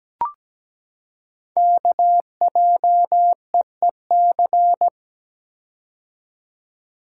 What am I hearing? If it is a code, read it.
Morse code, heard as a keyed tone, about 17 wpm, 700 Hz. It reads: KJEEC
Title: Code Groups - 5 Letters - 17wpm